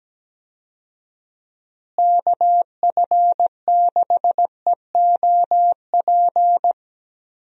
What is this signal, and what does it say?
Morse code, heard as a keyed tone, about 17 wpm, 700 Hz. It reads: KF6EOP
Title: USA Call Signs - 6 Characters - Repeated 3x - 17wpm